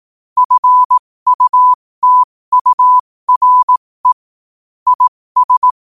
Morse code, heard as a keyed tone, about 18 wpm, spaced as for 15 wpm, 1000 Hz. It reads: FUTURE IS
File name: qotdfiles / 15wpm